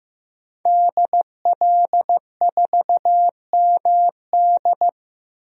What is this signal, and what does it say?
Morse code, heard as a keyed tone, about 15 wpm, 700 Hz. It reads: DL4MD
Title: DXCC-100 Call Signs - 5 Characters - Repeated 3x - 15wpm